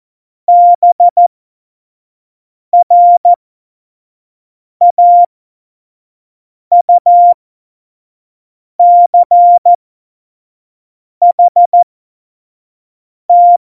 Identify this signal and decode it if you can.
Morse code, heard as a keyed tone, about 14 wpm, spaced as for 5 wpm, 700 Hz. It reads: BRAUCHT
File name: Morsekurs / Witze / Witz 15. Mai 2018, 25 F70